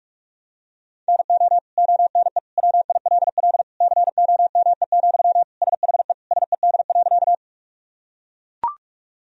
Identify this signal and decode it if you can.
Morse code, heard as a keyed tone, about 44 wpm, 700 Hz. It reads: NO ONE WILL COME, SHE SED.
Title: Sentences from Top 100 Words and Abbreviations 45wpm